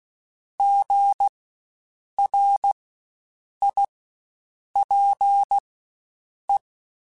Morse code, heard as a keyed tone, about 16 wpm, spaced as for 7 wpm, 800 Hz. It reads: GRIPE
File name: Spaced Alphas / Spaced Alphas mp3s / Gs sp @15